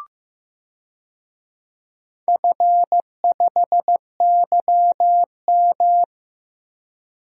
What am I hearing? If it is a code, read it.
Morse code, heard as a keyed tone, about 16 wpm, 700 Hz. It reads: F5YM
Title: DXCC-100 Call Signs - 4 Characters - Repeated 2x - 15wpm